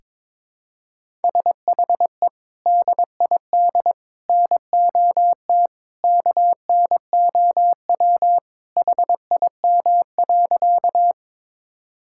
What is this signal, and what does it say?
Morse code, heard as a keyed tone, about 22 wpm, 700 Hz. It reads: SHE DID NOT KNOW HIM.
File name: practice / 22wpm / Sentences with 3-Word Phrases - 22wpm